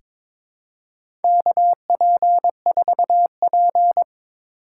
Morse code, heard as a keyed tone, about 22 wpm, 700 Hz. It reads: KP4P
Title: USA Call Signs - 4 Characters - 2x Character Spacing - 22wpm